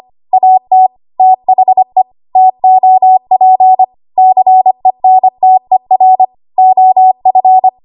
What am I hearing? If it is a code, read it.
Morse code, heard as a keyed tone, about 25 wpm, 750 Hz. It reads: AT THE TOP CENTER OF
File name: Archive / 25 WPM / 25WPM